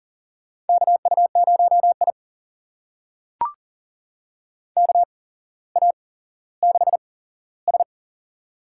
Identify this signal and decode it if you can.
Morse code, heard as a keyed tone, about 38 wpm, spaced as for 17 wpm, 700 Hz. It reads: KU0I KA6S
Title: USA Call Signs - 4 Characters - 8x Character Spacing - 40wpm